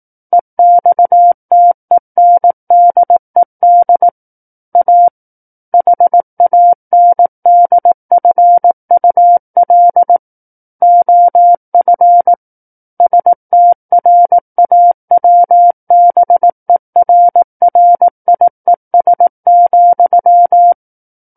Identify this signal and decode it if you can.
Morse code, heard as a keyed tone, about 18 wpm, 700 Hz. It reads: EXTENDED A HANDFUL OF STRAWBERRIES,
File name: 18 WPM